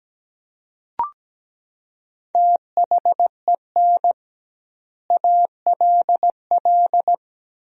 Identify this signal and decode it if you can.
Morse code, heard as a keyed tone, about 17 wpm, 700 Hz. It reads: THEN ALL
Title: 2x Word-Spacing - Sentences from Top 500 Words - 17wpm